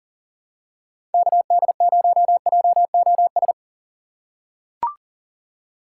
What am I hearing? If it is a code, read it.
Morse code, heard as a keyed tone, about 40 wpm, 700 Hz. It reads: KD0JOS
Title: USA Call Signs - 2x Character Spacing - 40wpm